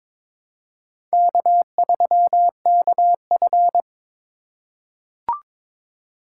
Morse code, heard as a keyed tone, about 22 wpm, 700 Hz. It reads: K3KF
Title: USA Call Signs - 4 Characters - 2x Character Spacing - 22wpm